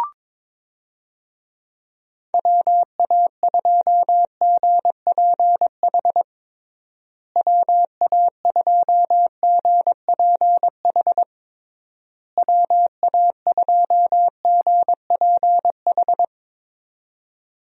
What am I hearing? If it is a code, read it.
Morse code, heard as a keyed tone, about 21 wpm, 700 Hz. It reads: WA2GPH WA2GPH WA2GPH
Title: USA Call Signs - 6 Characters - Repeated 3x - 22wpm